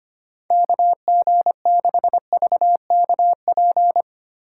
Morse code, heard as a keyed tone, about 25 wpm, 700 Hz. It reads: KG6VKP
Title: USA Call Signs - 6 Characters - 3x Character Spacing - 25wpm